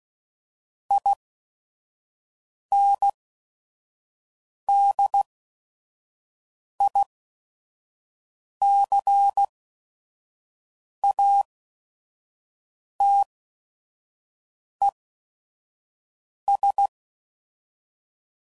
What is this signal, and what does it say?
Morse code, heard as a keyed tone, about 16 wpm, spaced as for 5 wpm, 800 Hz. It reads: INDICATES
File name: CW Code Groups @5WPM / Technical Subjects / XC